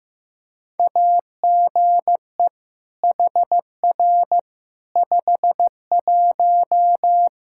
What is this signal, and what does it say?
Morse code, heard as a keyed tone, about 15 wpm, 700 Hz. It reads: AGE HR 51
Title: Practice QSO - Exchange 3 - More Info - 15wpm